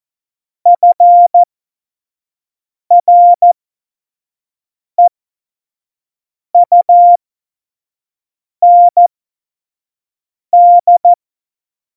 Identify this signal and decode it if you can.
Morse code, heard as a keyed tone, about 14 wpm, spaced as for 5 wpm, 700 Hz. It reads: FREUND
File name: Morsekurs / Witze / Witz 19. Dezember 2017, 25 F70